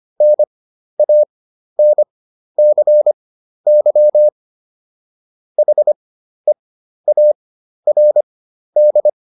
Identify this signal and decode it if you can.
Morse code, heard as a keyed tone, about 25 wpm, spaced as for 12 wpm, 600 Hz. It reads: NANCY HEARD